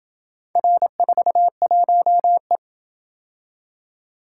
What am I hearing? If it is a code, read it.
Morse code, heard as a keyed tone, about 26 wpm, 700 Hz. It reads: R41E